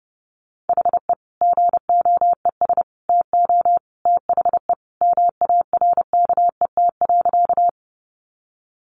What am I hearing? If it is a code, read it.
Morse code, heard as a keyed tone, about 30 wpm, 700 Hz. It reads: HE GOES TO THE MARKET.
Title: Additional Easy Phrases 30wpm